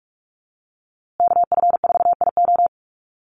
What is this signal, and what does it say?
Morse code, heard as a keyed tone, about 46 wpm, 700 Hz. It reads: KF4IO
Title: USA Call Signs Repeated 2x 45wpm